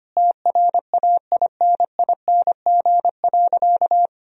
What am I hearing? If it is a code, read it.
Morse code, heard as a keyed tone, about 25 wpm, 700 Hz. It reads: TRAINING.
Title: Sentences from Top 1000 Words - Review Part 4 - 25wpm